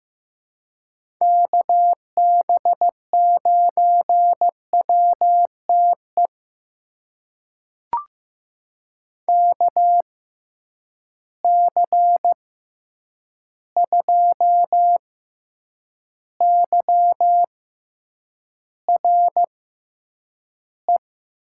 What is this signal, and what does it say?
Morse code, heard as a keyed tone, about 15 wpm, spaced as for 8 wpm, 700 Hz. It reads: KB9WTE KC2YRE
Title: USA Call Signs - 6 Characters - 6x Character Spacing - 15wpm